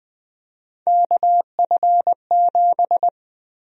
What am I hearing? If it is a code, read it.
Morse code, heard as a keyed tone, about 21 wpm, 700 Hz. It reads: KF7
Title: USA Call Sign Prefixes - 20wpm